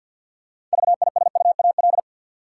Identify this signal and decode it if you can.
Morse code, heard as a keyed tone, about 53 wpm, 700 Hz. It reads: VISUAL
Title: Six Letter Words - 50wpm